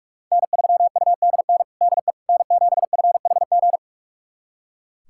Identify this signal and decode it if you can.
Morse code, heard as a keyed tone, about 41 wpm, 700 Hz. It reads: N3UDN DE N7FHG
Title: USA Callsign DE Callsign - 5 Characters - 45wpm